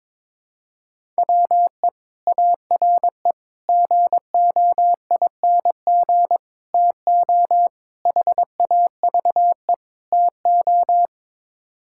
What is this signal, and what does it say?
Morse code, heard as a keyed tone, about 22 wpm, 700 Hz. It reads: WE ARE GOING TO HAVE TO
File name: practice / 22wpm / Sets of 6 Words - 22wpm